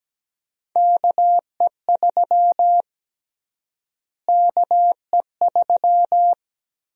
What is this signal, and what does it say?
Morse code, heard as a keyed tone, about 17 wpm, 700 Hz. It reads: KE3 KE3
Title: USA Call Sign Prefixes - Repeated 2x - 17wpm